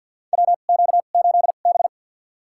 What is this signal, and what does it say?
Morse code, heard as a keyed tone, about 46 wpm, spaced as for 35 wpm, 700 Hz. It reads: WQ8B
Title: USA Call Signs - 4 Characters - 2x Character Spacing - 50wpm